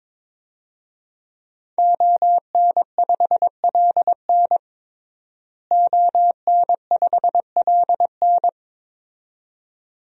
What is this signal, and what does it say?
Morse code, heard as a keyed tone, about 22 wpm, 700 Hz. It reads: ON5LN ON5LN
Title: DXCC-100 Call Signs - 5 Characters - Repeated 2x - 22wpm